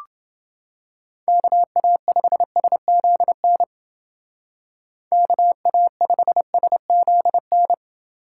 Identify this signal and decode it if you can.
Morse code, heard as a keyed tone, about 29 wpm, 700 Hz. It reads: KA5SZN KA5SZN